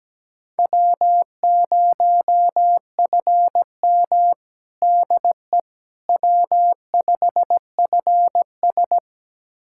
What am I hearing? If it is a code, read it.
Morse code, heard as a keyed tone, about 17 wpm, 700 Hz. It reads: W0FM DE W5FS
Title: USA Callsign DE Callsign - 4 Characters - 17wpm